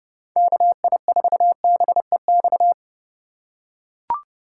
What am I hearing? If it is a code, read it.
Morse code, heard as a keyed tone, about 30 wpm, 700 Hz. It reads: KI4BEX
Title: USA Call Signs - 6 Characters - 2x Character Spacing - 30wpm